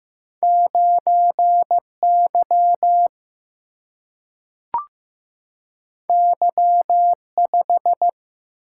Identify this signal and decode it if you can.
Morse code, heard as a keyed tone, about 15 wpm, 700 Hz. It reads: 9 Y Y 5
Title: Single-Letter Single-Number 15wpm